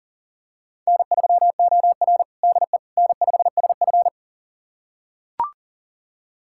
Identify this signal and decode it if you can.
Morse code, heard as a keyed tone, about 40 wpm, 700 Hz. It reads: N3OR DE N5SF
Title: USA Callsign DE Callsign - 4 Characters - 40wpm